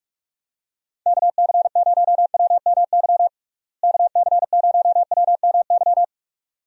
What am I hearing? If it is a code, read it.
Morse code, heard as a keyed tone, about 42 wpm, 700 Hz. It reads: KC0WMY KC0WMY
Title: USA Call Signs - 6 Characters - Repeated 2x - 45wpm